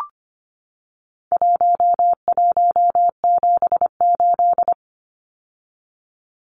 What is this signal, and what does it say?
Morse code, heard as a keyed tone, about 25 wpm, 700 Hz. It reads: 1178
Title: Numbers - 4-Digit 25wpm